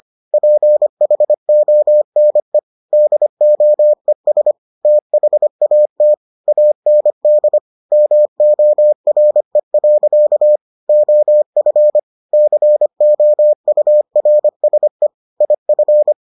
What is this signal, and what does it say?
Morse code, heard as a keyed tone, about 25 wpm, 600 Hz. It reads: PHONE DOES THAT AND MORE. OF COURSE IF